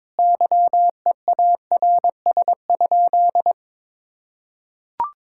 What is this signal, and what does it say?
Morse code, heard as a keyed tone, about 22 wpm, 700 Hz. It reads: YEARS?
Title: Sentences from Top 1000 Words - Review Part 6 - 22wpm